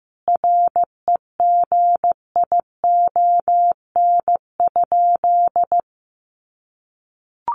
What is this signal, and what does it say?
Morse code, heard as a keyed tone, about 15 wpm, 700 Hz. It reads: REGION?
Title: Sentences from Top 1500 Words 15wpm